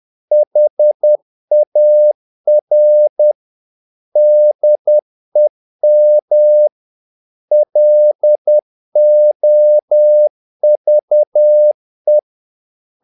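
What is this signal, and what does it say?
Morse code, heard as a keyed tone, about 10 wpm, 600 Hz. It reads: HAR DEM LOVE
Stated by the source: Morse practice of short Danish words at 10 wpm, from CW med Gnister.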